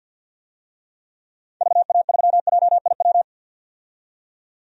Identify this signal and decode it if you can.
Morse code, heard as a keyed tone, about 45 wpm, 700 Hz. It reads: VA3JIW
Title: DXCC-100 Call Signs - 6 Characters - 50wpm